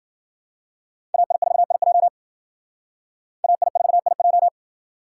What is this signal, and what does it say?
Morse code, heard as a keyed tone, about 51 wpm, 700 Hz. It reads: AI4IJ AI4IJ